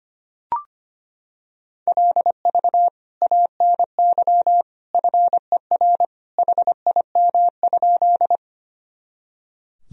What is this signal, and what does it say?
Morse code, heard as a keyed tone, about 25 wpm, 700 Hz. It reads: LV ANY FER HIM?
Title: Sentences from Top 400 Words and Abbreviations 25wpm